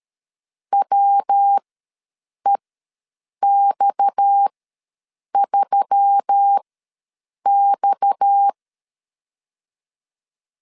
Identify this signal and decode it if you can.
Morse code, heard as a keyed tone, about 13 wpm, spaced as for 7 wpm, 800 Hz. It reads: WEX3X
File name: LEZIONI / 6 PKX0 / 6 PKX ws 12wpm 3106